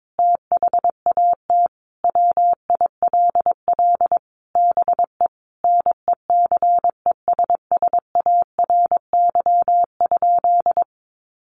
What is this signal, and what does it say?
Morse code, heard as a keyed tone, about 22 wpm, 700 Hz. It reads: THAT WILL BE NECESSARY?
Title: Sentences from Top 800 Words 22wpm